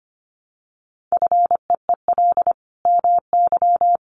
Morse code, heard as a keyed tone, about 25 wpm, 700 Hz. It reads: FEEL MY
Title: Sentences from Top 1100 Words 25wpm